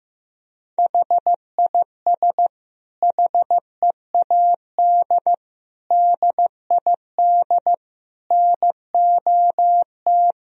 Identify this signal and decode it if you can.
Morse code, heard as a keyed tone, about 15 wpm, 700 Hz. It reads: HIS HEAD DID NOT
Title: Sentences from Top 1000 Words - Review Part 2 - 15wpm